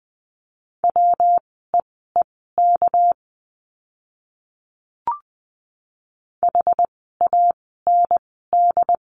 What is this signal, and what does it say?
Morse code, heard as a keyed tone, about 20 wpm, spaced as for 15 wpm, 700 Hz. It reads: WEEK HAND
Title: 230 Lesson 37f - Letter x - Words 20wpm